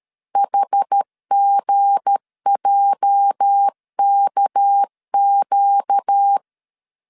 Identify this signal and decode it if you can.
Morse code, heard as a keyed tone, about 13 wpm, 800 Hz. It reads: HGJKQ
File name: LEZIONI / 8 YJ45 / 8 YJ 12wpm 2235